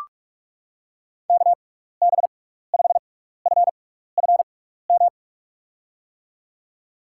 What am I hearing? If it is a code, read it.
Morse code, heard as a keyed tone, about 41 wpm, spaced as for 15 wpm, 700 Hz. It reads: KB5FFM
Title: USA Call Signs - 6 Characters - 6x Character Spacing - 45wpm